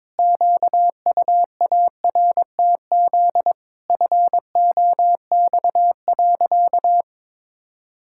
Morse code, heard as a keyed tone, about 22 wpm, 700 Hz. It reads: QUARTZ FOX.
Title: Pangram Challenge v2 - 22wpm